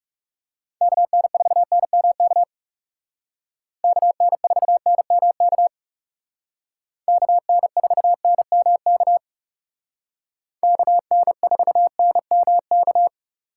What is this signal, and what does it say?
Morse code, heard as a keyed tone, about 38 wpm, 700 Hz. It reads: KN4NMK KN4NMK KN4NMK KN4NMK